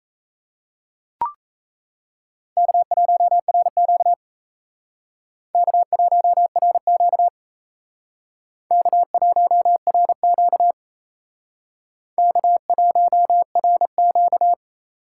Morse code, heard as a keyed tone, about 35 wpm, 700 Hz. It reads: K1RQ K1RQ K1RQ K1RQ
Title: Speed Racing - USA Call Signs 28wpm